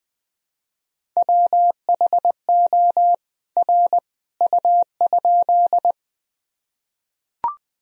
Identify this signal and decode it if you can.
Morse code, heard as a keyed tone, about 20 wpm, 700 Hz. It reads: WHO R U?
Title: Sentences from Top 100 Words and Abbreviations 20wpm